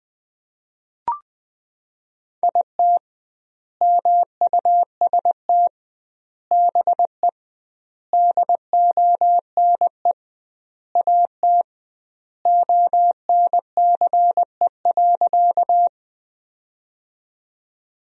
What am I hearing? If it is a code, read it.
Morse code, heard as a keyed tone, about 21 wpm, 700 Hz. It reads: IT MUST BE DONE AT ONCE.
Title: 2x Word-Spacing - Sentences from Top 500 Words - 20wpm